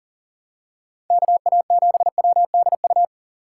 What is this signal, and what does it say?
Morse code, heard as a keyed tone, about 39 wpm, 700 Hz. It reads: KA7WDU
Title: USA Call Signs - 6 Characters - 3x Character Spacing - 40wpm